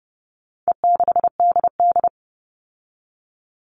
Morse code, heard as a keyed tone, about 30 wpm, 700 Hz. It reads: E6DD
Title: All Cut Numbers - 4-Digit 30wpm